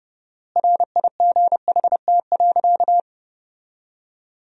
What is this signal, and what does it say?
Morse code, heard as a keyed tone, about 30 wpm, 700 Hz. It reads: RIGHT.
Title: 4x Word-Spacing - Sentences from Top 300 Words - 30wpm